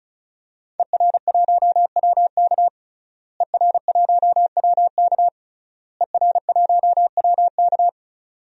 Morse code, heard as a keyed tone, about 34 wpm, 700 Hz. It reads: ER1WK ER1WK ER1WK
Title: DXCC-100 Call Signs - 5 Characters - Repeated 3x - 35wpm